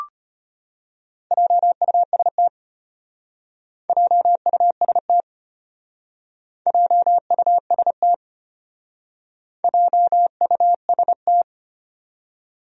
Morse code, heard as a keyed tone, about 32 wpm, 700 Hz. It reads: JUST JUST JUST JUST